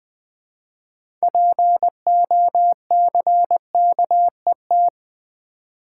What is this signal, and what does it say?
Morse code, heard as a keyed tone, about 20 wpm, 700 Hz. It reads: POCKET
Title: 6 Letter Words - First 3 Letters Given - 20wpm